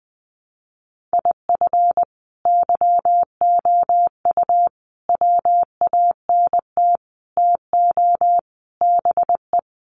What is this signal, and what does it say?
Morse code, heard as a keyed tone, about 20 wpm, 700 Hz. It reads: IF YOU WANT TO BE
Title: Sets of 5 Words Encore 20wpm